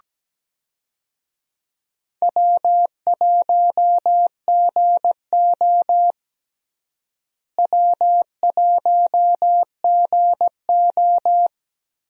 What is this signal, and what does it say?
Morse code, heard as a keyed tone, about 17 wpm, 700 Hz. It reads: W1GO W1GO
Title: USA Call Signs - 4 Characters - Repeated 3x - 17wpm